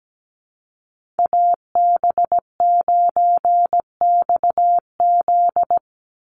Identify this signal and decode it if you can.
Morse code, heard as a keyed tone, about 17 wpm, 700 Hz. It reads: AB9XZ
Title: USA Call Signs 17wpm